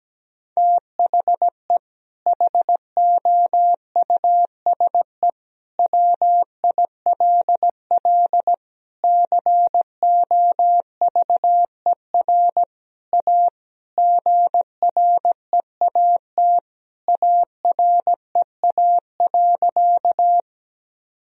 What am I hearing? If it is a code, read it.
Morse code, heard as a keyed tone, about 17 wpm, 700 Hz. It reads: THE HOUSE WILL COVER A GREAT AREA.
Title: Sentences from Top 1000 Words - Review Part 2 - 17wpm